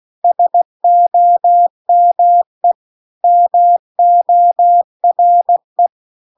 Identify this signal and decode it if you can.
Morse code, heard as a keyed tone, about 16 wpm, 700 Hz. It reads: SOME MORE